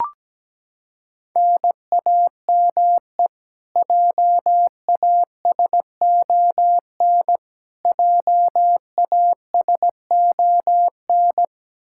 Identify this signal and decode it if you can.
Morse code, heard as a keyed tone, about 17 wpm, 700 Hz. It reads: NAME JASON JASON
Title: K1USN SST QSO Practice 17wpm